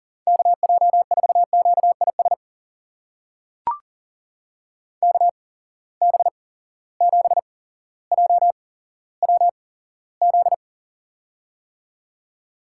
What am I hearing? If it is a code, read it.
Morse code, heard as a keyed tone, about 39 wpm, spaced as for 17 wpm, 700 Hz. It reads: KJ4QIS KB7JWZ
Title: USA Call Signs - 6 Characters - 8x Character Spacing - 40wpm